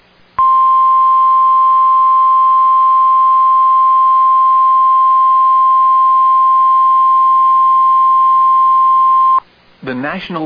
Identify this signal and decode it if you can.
Emergency Alert System (SAME) tones (the 1050 Hz attention tone). It is heard from 0.3 to 9.4 s.